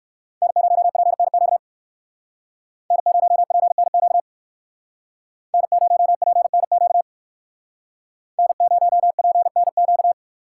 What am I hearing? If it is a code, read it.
Morse code, heard as a keyed tone, about 58 wpm, 700 Hz. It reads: N0PNQ N0PNQ N0PNQ N0PNQ